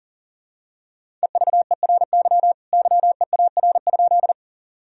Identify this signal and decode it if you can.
Morse code, heard as a keyed tone, about 37 wpm, 700 Hz. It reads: EVERY YEAR?
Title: Sentences from Top 1000 Words - Review Part 1 - 40wpm